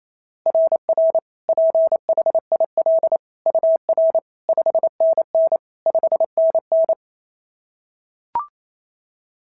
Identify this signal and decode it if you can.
Morse code, heard as a keyed tone, about 28 wpm, 690 Hz. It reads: RR PHIL UR 5NN 5NN
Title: SOTA QSO Practice 28wpm